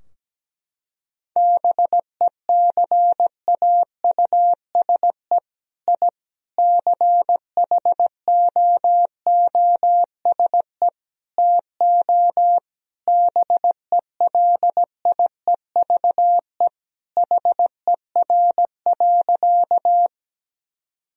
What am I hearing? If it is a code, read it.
Morse code, heard as a keyed tone, about 17 wpm, 700 Hz. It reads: BECAUSE I CHOOSE TO BELIEVE HER.
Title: Sentences from Top 1000 Words - Review Part 4 - 17wpm